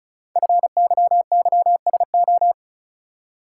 Morse code, heard as a keyed tone, about 36 wpm, 700 Hz. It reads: FYYSO
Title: Code Groups - 5 Letters - 35wpm